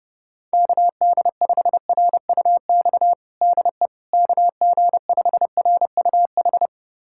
Morse code, heard as a keyed tone, about 32 wpm, 700 Hz. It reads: KD5RUX DE KG5RUH
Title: USA Callsign DE Callsign - 6 Characters - 30wpm